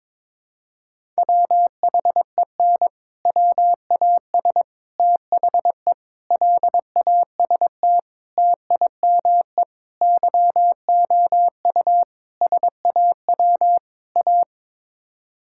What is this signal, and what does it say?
Morse code, heard as a keyed tone, about 21 wpm, 700 Hz. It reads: WHEN WAS THE LAST TIME YOU SAW A
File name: practice / 22wpm / Sets of 8 Words - 22wpm